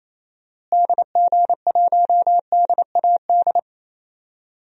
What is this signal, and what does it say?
Morse code, heard as a keyed tone, about 27 wpm, 700 Hz. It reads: DG1DAD
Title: DXCC-100 Call Signs - 6 Characters - Repeated 2x - 28wpm